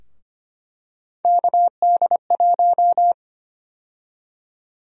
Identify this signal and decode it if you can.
Morse code, heard as a keyed tone, about 25 wpm, 700 Hz. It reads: KD1
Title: USA Call Sign Prefixes - Repeated 2x - 25wpm